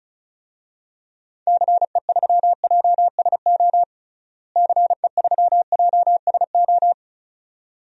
Morse code, heard as a keyed tone, about 34 wpm, 700 Hz. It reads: CE3JSO CE3JSO
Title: DXCC-100 Call Signs - 6 Characters - Repeated 2x - 35wpm